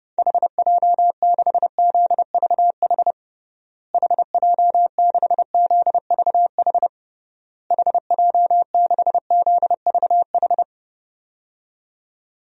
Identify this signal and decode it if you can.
Morse code, heard as a keyed tone, about 30 wpm, 700 Hz. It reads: HJ6ZVH HJ6ZVH HJ6ZVH
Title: DXCC-100 Call Signs - 6 Characters - Repeated 3x - 30wpm